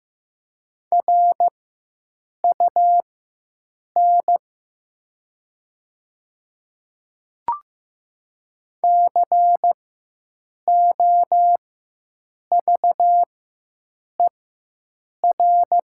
Morse code, heard as a keyed tone, about 15 wpm, spaced as for 7 wpm, 700 Hz. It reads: RUN COVER
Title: 4x Character Spacing - Top 300 Words - 15wpm